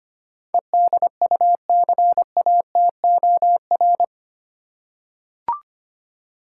Morse code, heard as a keyed tone, about 25 wpm, 700 Hz. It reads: EDUCATOR
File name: Eight Letter Words - 25wpm